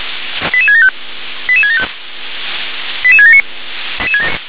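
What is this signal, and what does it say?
Five-tone selective call (ZVEI), heard as quick runs of five tones, tones of about 70 ms. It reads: 88675 88675 88679
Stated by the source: ZVEI five-tone selective call 88670, received with an SDR.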